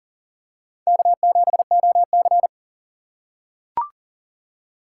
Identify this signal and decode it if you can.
Morse code, heard as a keyed tone, about 40 wpm, 700 Hz. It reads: K7OC
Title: USA Call Signs - 4 Characters - 6x Character Spacing - 40wpm